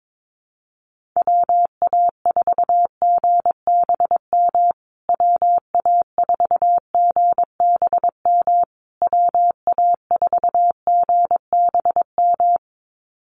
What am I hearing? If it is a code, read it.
Morse code, heard as a keyed tone, about 22 wpm, 700 Hz. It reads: WA4GBM WA4GBM WA4GBM
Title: USA Call Signs Repeated 3x 22wpm